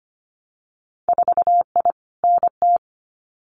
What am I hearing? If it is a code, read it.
Morse code, heard as a keyed tone, about 25 wpm, 690 Hz. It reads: 4I NT
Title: Winter Field Day Exchanges Encore 25wpm